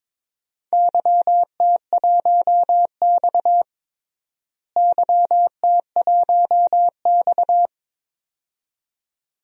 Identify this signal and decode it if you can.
Morse code, heard as a keyed tone, about 21 wpm, 700 Hz. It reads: YT1X YT1X
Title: DXCC-100 Call Signs - 4 Characters - Repeated 2x - 22wpm